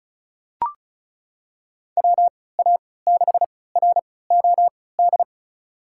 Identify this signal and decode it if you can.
Morse code, heard as a keyed tone, about 35 wpm, spaced as for 20 wpm, 700 Hz. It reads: WA6ROD
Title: USA Call Signs - 6 Characters - 3x Character Spacing - 35wpm